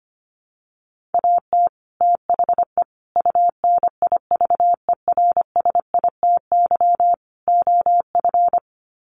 Morse code, heard as a keyed tone, about 25 wpm, 700 Hz. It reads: AT THE UNIVERSITY OF
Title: Sets of 4 Words 25wpm